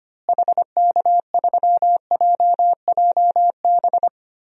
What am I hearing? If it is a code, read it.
Morse code, heard as a keyed tone, about 25 wpm, 700 Hz. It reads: HK3JJB
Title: DXCC-100 Call Signs - 6 Characters - Repeated 2x - 25wpm